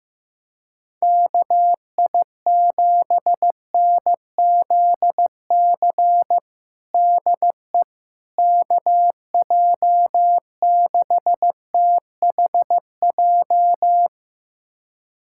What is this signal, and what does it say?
Morse code, heard as a keyed tone, about 15 wpm, 700 Hz. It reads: KI7NZC DE KJ6THJ
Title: USA Callsign DE Callsign - 6 Characters - 15wpm